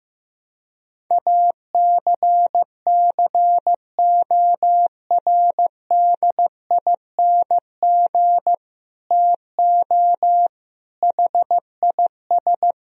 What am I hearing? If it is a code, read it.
Morse code, heard as a keyed tone, about 15 wpm, 700 Hz. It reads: ACCORDING TO HIS
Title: Sentences from Top 1000 Words - Review Part 4 - 15wpm